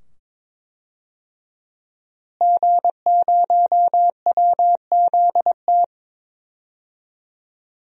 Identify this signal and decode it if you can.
Morse code, heard as a keyed tone, about 22 wpm, 700 Hz. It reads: G0WZT
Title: DXCC-100 Call Signs - 5 Characters - Repeated 2x - 22wpm